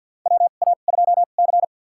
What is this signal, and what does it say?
Morse code, heard as a keyed tone, about 53 wpm, spaced as for 37 wpm, 700 Hz. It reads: WA2C
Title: USA Call Signs - 4 Characters - 2x Character Spacing - 50wpm